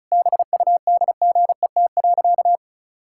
Morse code, heard as a keyed tone, about 34 wpm, 700 Hz. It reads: BUDGET.
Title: Pangram Challenge v2 - 35wpm